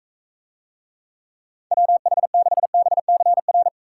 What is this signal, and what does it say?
Morse code, heard as a keyed tone, about 45 wpm, 700 Hz. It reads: WH6BCR